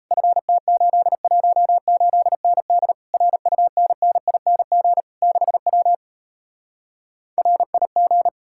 Morse code, heard as a keyed tone, about 37 wpm, 700 Hz. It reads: FT818ND RUNNING 6W RIG